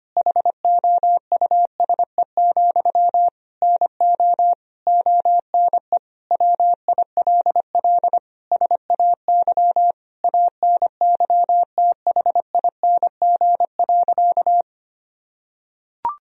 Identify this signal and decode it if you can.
Morse code, heard as a keyed tone, about 25 wpm, 700 Hz. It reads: HOUSE, NO ONE WILL SAY ANYTHING.
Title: Sentences from Top 1000 Words - Review Part 2 - 25wpm